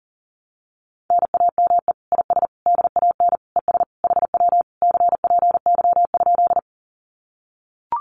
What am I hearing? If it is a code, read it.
Morse code, heard as a keyed tone, about 40 wpm, 700 Hz. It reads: NAME IS DAN ES HW CPY?
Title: Common QSO Phrases Encore 40wpm